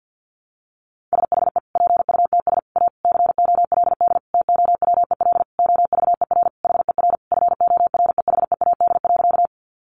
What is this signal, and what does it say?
Morse code, heard as a keyed tone, about 50 wpm, 700 Hz. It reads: SHE PUTS A COLD TOWEL OVER HER FOREHEAD.
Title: Easy Phrases 50wpm